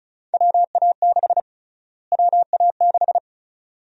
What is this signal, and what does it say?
Morse code, heard as a keyed tone, about 36 wpm, 700 Hz. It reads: WA6 WA6
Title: USA Call Sign Prefixes - Repeated 2x - 35wpm